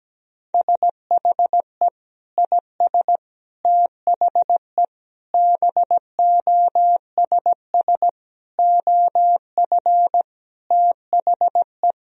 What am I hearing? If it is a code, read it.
Morse code, heard as a keyed tone, about 17 wpm, 700 Hz. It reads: SHE IS THE BOSS OF THE
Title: Sentences from Top 1000 Words - Review Part 4 - 17wpm